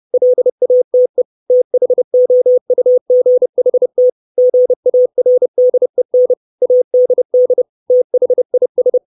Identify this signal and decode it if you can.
Morse code, heard as a keyed tone, about 30 wpm, 500 Hz. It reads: LATE THOUGHT GARDEN ADD THIS
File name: 632 Words 30WPM